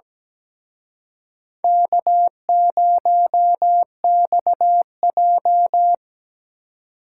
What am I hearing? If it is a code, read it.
Morse code, heard as a keyed tone, about 20 wpm, 700 Hz. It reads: K0XJ
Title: Speed Racing - USA 1x2 Call Signs 17wpm